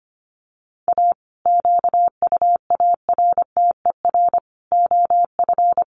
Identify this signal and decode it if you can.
Morse code, heard as a keyed tone, about 25 wpm, 700 Hz. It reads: A QUARTER OF
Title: Sentences from Top 1300 Words 25wpm